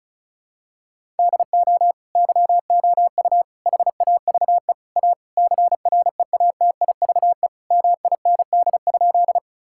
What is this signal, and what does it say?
Morse code, heard as a keyed tone, about 36 wpm, 700 Hz. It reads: DO YOU HAVE A CREATIVE MIND?
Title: Sentences from Top 1000 Words - Review Part 5 - 35wpm